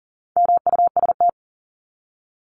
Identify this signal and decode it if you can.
Morse code, heard as a keyed tone, about 40 wpm, 700 Hz. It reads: MUST
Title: Top 500 Words 40wpm